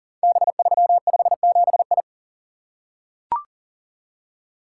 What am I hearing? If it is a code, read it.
Morse code, heard as a keyed tone, about 39 wpm, 700 Hz. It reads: B357I
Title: Code Groups - 5 Letter:Numbers - 40wpm